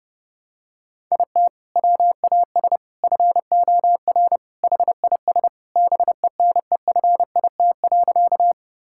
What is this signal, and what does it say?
Morse code, heard as a keyed tone, about 30 wpm, 700 Hz. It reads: IT WAS FOR HIS BENEFIT.
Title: Sentences from Top 1000 Words - Review Part 6 - 30wpm